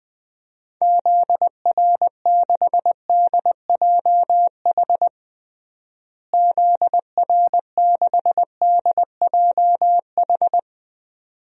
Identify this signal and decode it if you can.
Morse code, heard as a keyed tone, about 20 wpm, 700 Hz. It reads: ZR6DJH ZR6DJH
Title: DXCC-100 Call Signs - 6 Characters - Repeated 2x - 20wpm